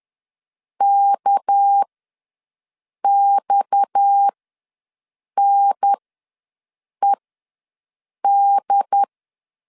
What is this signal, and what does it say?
Morse code, heard as a keyed tone, about 10 wpm, spaced as for 6 wpm, 800 Hz. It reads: KXNED
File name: LEZIONI / 6 PKX0 / 6 PKX0 ws 10wpm 4220